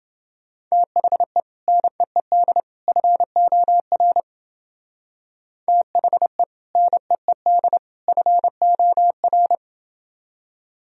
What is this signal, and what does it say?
Morse code, heard as a keyed tone, about 31 wpm, 700 Hz. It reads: THE NEED FOR THE NEED FOR